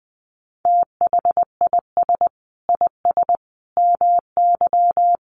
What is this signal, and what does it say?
Morse code, heard as a keyed tone, about 19 wpm, 700 Hz. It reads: THIS IS MY
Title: Sentences from Top 300 Words 20wpm